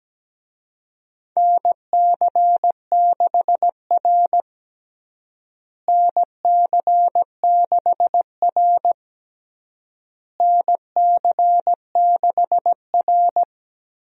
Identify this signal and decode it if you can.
Morse code, heard as a keyed tone, about 17 wpm, 700 Hz. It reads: NC6R NC6R NC6R
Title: USA Call Signs - 4 Characters - Repeated 3x - 17wpm